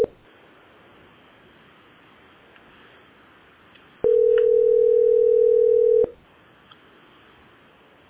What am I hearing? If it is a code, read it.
Ringback tone (North America), 440 + 480 Hz.